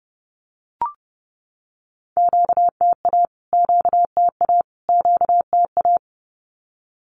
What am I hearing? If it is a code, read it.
Morse code, heard as a keyed tone, about 30 wpm, 700 Hz. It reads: QTA QTA QTA
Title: Q-Codes Repeated 3x 30wpm